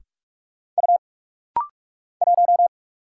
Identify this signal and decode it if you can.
Morse code, heard as a keyed tone, about 44 wpm, 700 Hz. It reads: U 1
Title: Single Letter-Number Alt - Rapid-Fire 45wpm